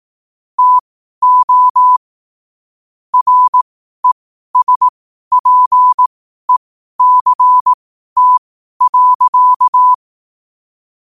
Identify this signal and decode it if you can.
Morse code, heard as a keyed tone, about 18 wpm, spaced as for 12 wpm, 1000 Hz. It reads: TO RESPECT.
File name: qotdfiles / 12wpm